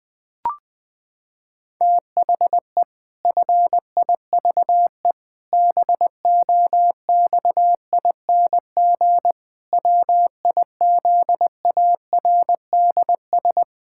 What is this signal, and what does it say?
Morse code, heard as a keyed tone, about 21 wpm, 700 Hz. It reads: THE FIVE BOXING WIZARDS
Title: Pangram Challenge v2 - 20wpm